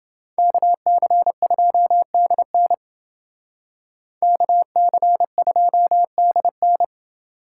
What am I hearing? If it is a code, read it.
Morse code, heard as a keyed tone, about 29 wpm, 700 Hz. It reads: KC2DN KC2DN